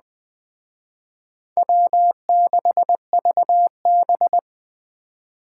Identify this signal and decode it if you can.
Morse code, heard as a keyed tone, about 20 wpm, 700 Hz. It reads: W6VB
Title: Speed Racing - USA Call Signs 17wpm